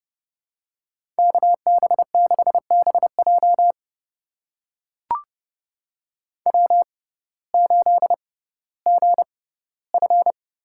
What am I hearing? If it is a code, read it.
Morse code, heard as a keyed tone, about 31 wpm, spaced as for 17 wpm, 700 Hz. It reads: KB6BJ W8GF
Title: USA Call Signs - 5 Characters - 6x Character Spacing - 30wpm